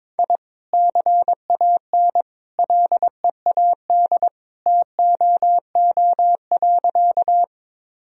Morse code, heard as a keyed tone, about 22 wpm, 700 Hz. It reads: I CAN LEAD TOO.
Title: Sentences from Top 1000 Words - Review Part 2 - 22wpm